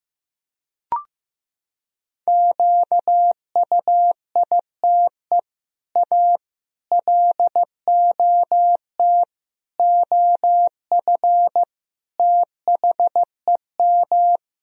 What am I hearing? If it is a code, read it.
Morse code, heard as a keyed tone, about 14 wpm, 700 Hz. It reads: QUITE A LOT OF THEM
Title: Sentences from Top 1000 Words - Review Part 3 - 15wpm